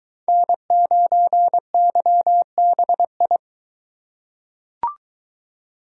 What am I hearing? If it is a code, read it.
Morse code, heard as a keyed tone, about 22 wpm, 700 Hz. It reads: N9YBI